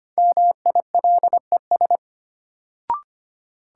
Morse code, heard as a keyed tone, about 25 wpm, 700 Hz. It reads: MILES
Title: Sentences from Top 1000 Words - Review Part 3 - New Word Drill - 25wpm